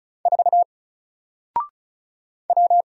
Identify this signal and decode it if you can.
Morse code, heard as a keyed tone, about 35 wpm, 700 Hz. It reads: 4 W
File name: Single-Letter-Number-35wpm